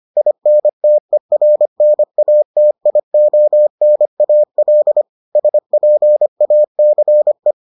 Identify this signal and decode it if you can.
Morse code, heard as a keyed tone, about 25 wpm, 600 Hz. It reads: INTERNATIONAL SPACE